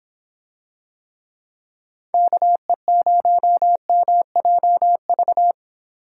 Morse code, heard as a keyed tone, about 26 wpm, 700 Hz. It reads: KE0MJV